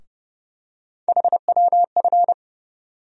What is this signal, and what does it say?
Morse code, heard as a keyed tone, about 30 wpm, 700 Hz. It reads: HWF
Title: USA Call Sign Postfixes - 3 Letters 30wpm